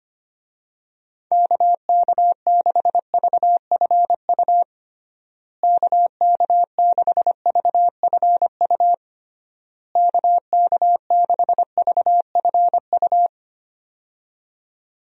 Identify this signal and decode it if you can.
Morse code, heard as a keyed tone, about 25 wpm, 700 Hz. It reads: KK6VFU KK6VFU KK6VFU
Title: USA Call Signs - 6 Characters - Repeated 3x - 25wpm